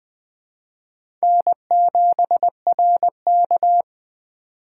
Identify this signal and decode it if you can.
Morse code, heard as a keyed tone, about 21 wpm, 700 Hz. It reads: N7RK